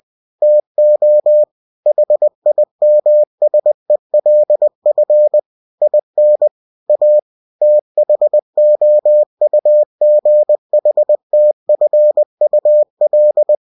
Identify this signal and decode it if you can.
Morse code, heard as a keyed tone, about 20 wpm, 600 Hz. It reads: TO HIMSELF IN A THOUGHTFUL
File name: cw-books / winnie-the-pooh / winnie-20wpm-page-0037